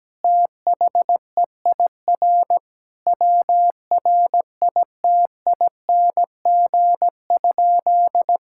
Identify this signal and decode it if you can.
Morse code, heard as a keyed tone, about 17 wpm, 700 Hz. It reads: THEIR WRITING?
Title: Sentences from Top 1000 Words - Review Part 6 - 17wpm